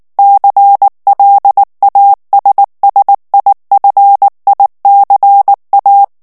Morse code, heard as a keyed tone, about 19 wpm, 800 Hz. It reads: CLASSIFICA
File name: morse mp3 / ITALIEN texte clair 2 vitesse 18 mots minute